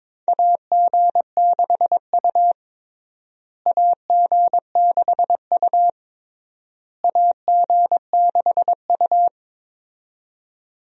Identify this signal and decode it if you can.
Morse code, heard as a keyed tone, about 22 wpm, 700 Hz. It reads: AG6U AG6U AG6U
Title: USA Call Signs - 4 Characters - Repeated 3x - 22wpm